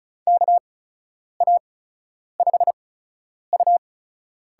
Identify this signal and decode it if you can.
Morse code, heard as a keyed tone, about 36 wpm, spaced as for 10 wpm, 700 Hz. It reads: KA5U
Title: USA Call Signs - 6 Characters - 8x Character Spacing - 35wpm